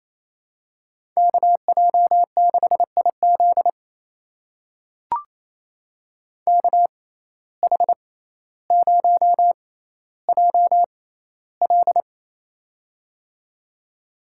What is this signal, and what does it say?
Morse code, heard as a keyed tone, about 28 wpm, spaced as for 14 wpm, 700 Hz. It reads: KJ6IZ KH0JL
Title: USA Call Signs - 5 Characters - 6x Character Spacing - 28wpm